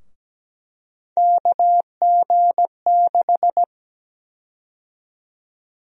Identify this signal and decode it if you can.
Morse code, heard as a keyed tone, about 17 wpm, 700 Hz. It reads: KG6
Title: USA Call Sign Prefixes - Repeated 2x - 17wpm